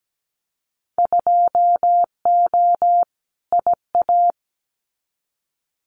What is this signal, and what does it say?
Morse code, heard as a keyed tone, about 17 wpm, 700 Hz. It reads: 2O IA
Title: Winter Field Day Exchanges 17wpm